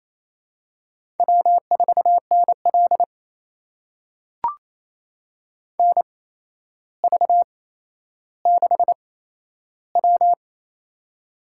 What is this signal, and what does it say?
Morse code, heard as a keyed tone, about 28 wpm, spaced as for 12 wpm, 700 Hz. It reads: W4NL NV6W
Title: USA Call Signs - 8x Character Spacing - 28wpm